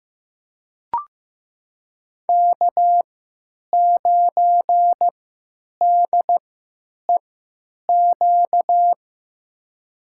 Morse code, heard as a keyed tone, about 15 wpm, spaced as for 9 wpm, 700 Hz. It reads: K9DEQ
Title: USA Call Signs - 5 Characters - 3x Character Spacing - 15wpm